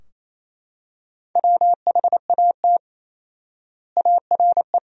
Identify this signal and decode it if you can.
Morse code, heard as a keyed tone, about 28 wpm, 700 Hz. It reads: WHAT ARE
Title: 4x Word-Spacing - Sentences from Top 100 Words - 28wpm